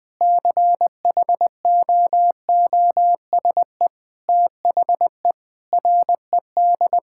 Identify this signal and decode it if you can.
Morse code, heard as a keyed tone, about 20 wpm, 700 Hz. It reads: CHOOSE THE RED
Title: Sentences with 2-Word Phrases - 20wpm